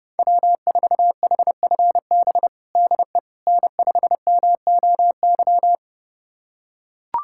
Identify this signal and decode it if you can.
Morse code, heard as a keyed tone, about 30 wpm, 700 Hz. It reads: W4HFB DE N5MOY
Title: USA Callsign DE Callsign - 5 Characters - 30wpm